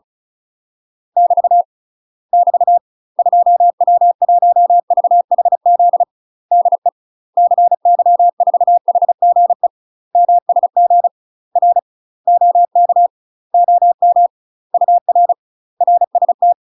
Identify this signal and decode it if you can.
Morse code, heard as a keyed tone, about 36 wpm, 700 Hz. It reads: <BT> <BT> 2W1VHZ DE CY4HGE MSG R OK OM UR RST
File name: UK MASTER QSOs TRAINING / 35WPM / UK MASTER QSOs 0002